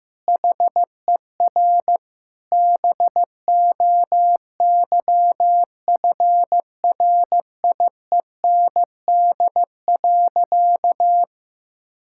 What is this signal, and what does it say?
Morse code, heard as a keyed tone, about 15 wpm, 700 Hz. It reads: HER BOYFRIEND.
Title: Sentences from Top 1000 Words - Review Part 5 - 15wpm